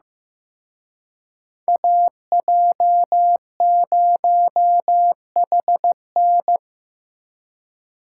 Morse code, heard as a keyed tone, about 17 wpm, 700 Hz. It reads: AJ0HN
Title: Speed Racing - USA 2x2 Call Signs 15wpm